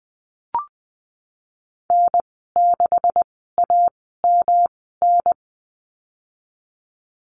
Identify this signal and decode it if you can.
Morse code, heard as a keyed tone, about 20 wpm, spaced as for 15 wpm, 700 Hz. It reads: N6AMN
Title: 173 Lesson 28g - Number 9 - Call Signs 20wpm